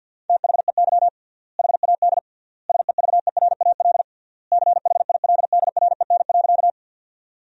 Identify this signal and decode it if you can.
Morse code, heard as a keyed tone, about 52 wpm, 700 Hz. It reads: THEY HAD SEVERAL CHILDREN.
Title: 3x Word-Spacing - Sentences from Top 400 Words - 50wpm